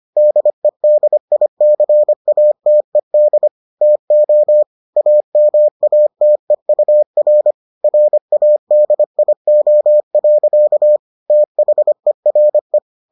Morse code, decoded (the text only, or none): DEDICATED TO AMATEUR RADIO. THERE